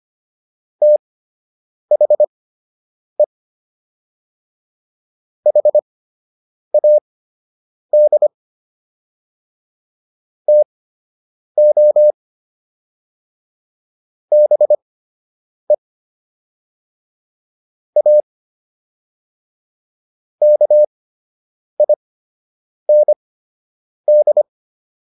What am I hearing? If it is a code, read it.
Morse code, heard as a keyed tone, about 25 wpm, spaced as for 8 wpm, 600 Hz. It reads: THE HAD TO BE A KIND